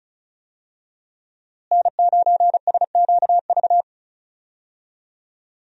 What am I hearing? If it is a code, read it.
Morse code, heard as a keyed tone, about 36 wpm, 700 Hz. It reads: N9SQV
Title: USA Call Signs - 5 Characters - Repeated 2x - 35wpm